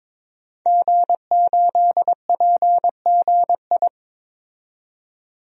Morse code, heard as a keyed tone, about 21 wpm, 700 Hz. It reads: G8PGI
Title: DXCC-100 Call Signs - 5 Characters - 22wpm